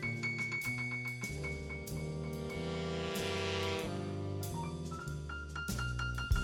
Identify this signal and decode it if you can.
Dial-up modem handshake.